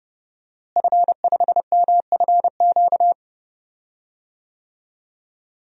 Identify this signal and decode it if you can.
Morse code, heard as a keyed tone, about 31 wpm, 700 Hz. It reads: F5MFQ
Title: DXCC-100 Call Signs - 5 Characters - Repeated 2x - 30wpm